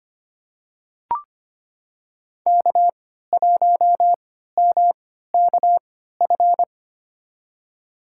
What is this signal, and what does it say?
Morse code, heard as a keyed tone, about 25 wpm, spaced as for 14 wpm, 700 Hz. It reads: K1MKF
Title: USA Call Signs - 5 Characters - 3x Character Spacing - 25wpm